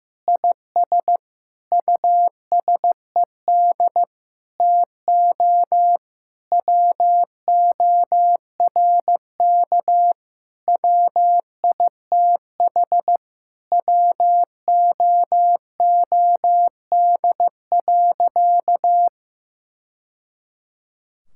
Morse code, heard as a keyed tone, about 15 wpm, 700 Hz. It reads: IS USED TO WORK WITH WOOD.
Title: Sentences from Top 1000 Words - Review Part 6 - 15wpm